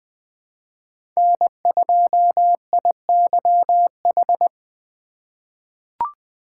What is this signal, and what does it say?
Morse code, heard as a keyed tone, about 20 wpm, 700 Hz. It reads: N2IYH
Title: USA Call Signs - 3x Character Spacing - 20wpm